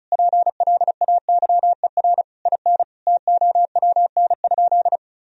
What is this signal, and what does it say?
Morse code, heard as a keyed tone, about 36 wpm, 700 Hz. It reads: PLAYER IN TOWN?
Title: Sentences from Top 1000 Words - Review Part 4 - 35wpm